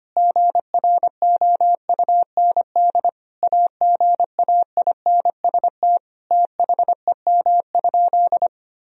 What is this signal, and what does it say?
Morse code, heard as a keyed tone, about 25 wpm, 700 Hz. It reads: GROUND AGAINST THEM?
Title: Sentences from Top 1000 Words - Review Part 2 - 25wpm